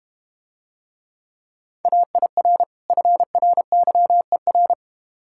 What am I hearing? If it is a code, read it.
Morse code, heard as a keyed tone, about 32 wpm, 700 Hz. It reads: AIR FRYER